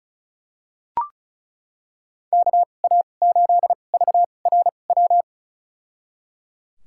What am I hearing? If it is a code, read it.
Morse code, heard as a keyed tone, about 36 wpm, spaced as for 25 wpm, 700 Hz. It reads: KA8VRW
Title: USA Call Signs - 2x Character Spacing - 35wpm